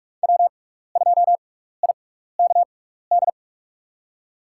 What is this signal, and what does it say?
Morse code, heard as a keyed tone, about 45 wpm, spaced as for 15 wpm, 700 Hz. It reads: W2IKD
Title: USA Call Signs - 5 Characters - 6x Character Spacing - 45wpm